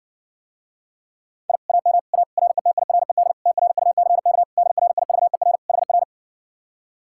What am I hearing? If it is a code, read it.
Morse code, heard as a keyed tone, about 61 wpm, 700 Hz. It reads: I AM A RETIRED TRUCK DRIVER HR